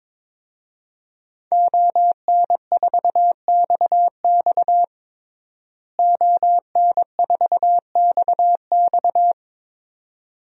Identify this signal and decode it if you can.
Morse code, heard as a keyed tone, about 21 wpm, 700 Hz. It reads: ON4XX ON4XX
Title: DXCC-100 Call Signs - 5 Characters - Repeated 2x - 22wpm